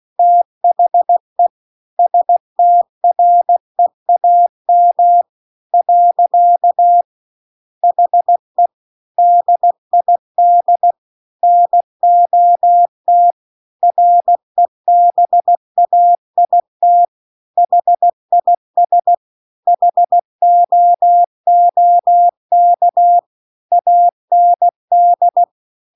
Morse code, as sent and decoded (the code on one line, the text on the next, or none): - .... / ... - .-. .- -- / .-.-.- / .... / -.. .. -.. / -. --- - / .-. -... .- .. - / .... .. ... / .... --- --- -.- / .- -. -..
THE STREAM . HE DID NOT REBAIT HIS HOOK AND